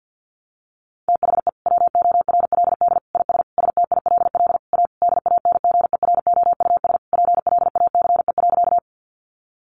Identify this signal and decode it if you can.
Morse code, heard as a keyed tone, about 50 wpm, 700 Hz. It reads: THE WORLD IS STILL A DANGEROUS PLACE.
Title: Sentences from Top 1200 Words 50wpm